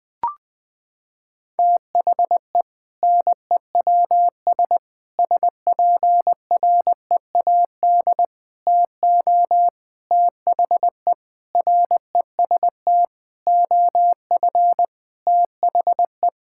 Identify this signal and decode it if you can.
Morse code, heard as a keyed tone, about 20 wpm, 700 Hz. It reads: THE NEWS SPREAD TO THE REST OF THE
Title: Sentences with 5-Word Phrases - 20wpm